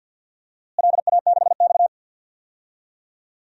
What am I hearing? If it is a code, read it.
Morse code, heard as a keyed tone, about 53 wpm, 700 Hz. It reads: LA6X
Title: DXCC-100 Call Signs - 4 Characters - Repeated 3x - 50wpm